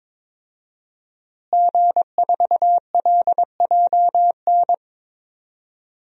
Morse code, heard as a keyed tone, about 21 wpm, 700 Hz. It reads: G4LJN
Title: DXCC-100 Call Signs - 5 Characters - Repeated 2x - 22wpm